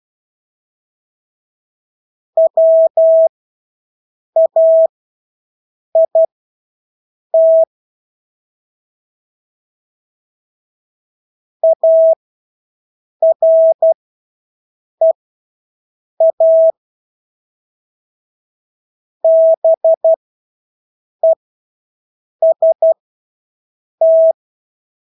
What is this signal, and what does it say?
Morse code, heard as a keyed tone, about 12 wpm, spaced as for 6 wpm, 650 Hz. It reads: WAIT AREA BEST
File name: Engelske ord 12-6 wpm